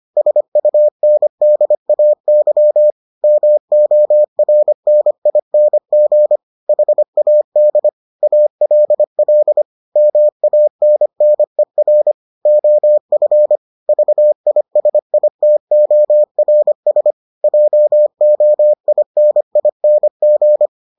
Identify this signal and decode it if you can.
Morse code, heard as a keyed tone, about 25 wpm, 600 Hz. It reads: SUNDAY MORNING HAD ALL MANNER OF VISITORS JOINING